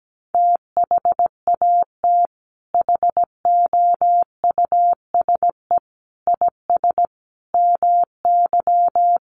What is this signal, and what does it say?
Morse code, heard as a keyed tone, about 17 wpm, 700 Hz. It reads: THAT HOUSE IS MY
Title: Sentences from Top 1000 Words 17wpm